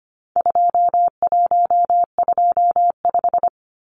Morse code, heard as a keyed tone, about 25 wpm, 700 Hz. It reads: 2125
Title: Numbers - 4-Digit 25wpm